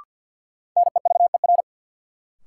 Morse code, heard as a keyed tone, about 53 wpm, 700 Hz. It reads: NEVER
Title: Sentences from Top 300 Words - New Word Drill - 50wpm